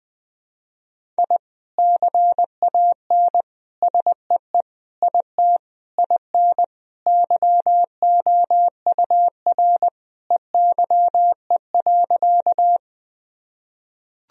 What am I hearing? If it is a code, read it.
Morse code, heard as a keyed tone, about 20 wpm, 700 Hz. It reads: I CAN SEE IT IN YOUR EYE.
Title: Sentences from Top 1000 Words - Review Part 1 - 20wpm